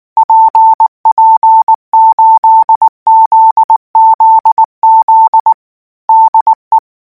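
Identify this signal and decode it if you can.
Morse code, heard as a keyed tone, about 19 wpm, 890 Hz. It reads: PP8ZZZ DE
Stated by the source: Morse code practice text at 18 wpm, from Portal do CW.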